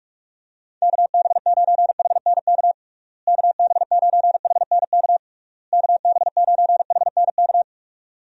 Morse code, heard as a keyed tone, about 46 wpm, 700 Hz. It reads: KB9HNK KB9HNK KB9HNK
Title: USA Call Signs - 6 Characters - Repeated 3x - 45wpm